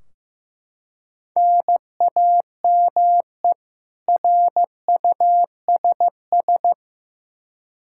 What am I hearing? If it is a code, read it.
Morse code, heard as a keyed tone, about 15 wpm, 700 Hz. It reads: NAME RUSS
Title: K1USN SST QSO Practice 15wpm